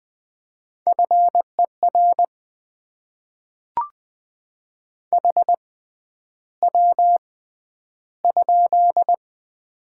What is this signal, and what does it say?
Morse code, heard as a keyed tone, about 20 wpm, spaced as for 10 wpm, 690 Hz. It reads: FER HW?
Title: Top 20 QSO Elements - 6x Character Spacing - 20wpm